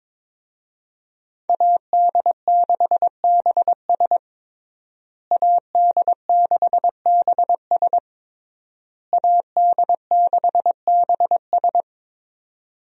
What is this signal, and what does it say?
Morse code, heard as a keyed tone, about 22 wpm, 700 Hz. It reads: AD6BS AD6BS AD6BS
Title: USA Call Signs - 5 Characters - Repeated 3x - 22wpm